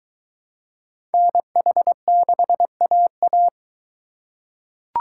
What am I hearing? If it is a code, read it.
Morse code, heard as a keyed tone, about 20 wpm, 700 Hz. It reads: NH6AA